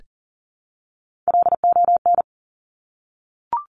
Morse code, heard as a keyed tone, about 41 wpm, 690 Hz. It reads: LON
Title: Top Three Letter Combinations in Words 40wpm